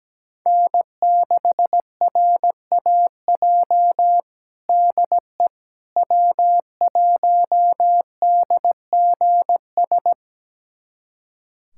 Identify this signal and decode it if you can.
Morse code, heard as a keyed tone, about 17 wpm, 700 Hz. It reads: N6RAJ DE W1DGS
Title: USA Callsign DE Callsign - 5 Characters - 17wpm